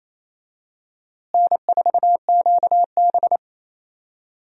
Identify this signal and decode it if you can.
Morse code, heard as a keyed tone, about 28 wpm, 700 Hz. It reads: N4QB
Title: USA Call Signs - 4 Characters - Repeated 2x - 28wpm